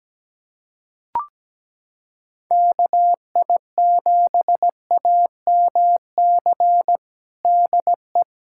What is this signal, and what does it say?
Morse code, heard as a keyed tone, about 17 wpm, 700 Hz. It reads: KI7AMC DE
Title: USA Callsign DE Callsign 17wpm